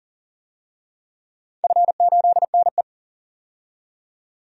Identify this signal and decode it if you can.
Morse code, heard as a keyed tone, about 38 wpm, 700 Hz. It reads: F8NE
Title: DXCC-100 Call Signs - 4 Characters - 40wpm